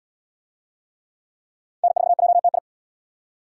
Morse code, heard as a keyed tone, about 71 wpm, 700 Hz. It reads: N4JTI